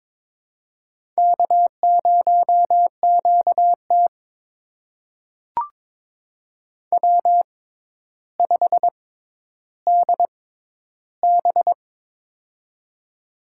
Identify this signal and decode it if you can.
Morse code, heard as a keyed tone, about 22 wpm, spaced as for 11 wpm, 700 Hz. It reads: K0QT W5DB
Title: USA Call Signs - 4 Characters - 6x Character Spacing - 22wpm